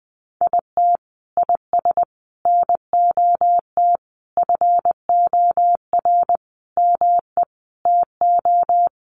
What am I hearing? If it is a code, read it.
Morse code, heard as a keyed tone, about 20 wpm, 700 Hz. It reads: IT IS NOT FOR ME TO
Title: Sentences from Top 600 Words 20wpm